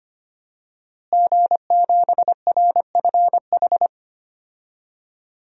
Morse code, heard as a keyed tone, about 25 wpm, 700 Hz. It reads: G7RFH
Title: DXCC-100 Call Signs - 5 Characters - Repeated 2x - 25wpm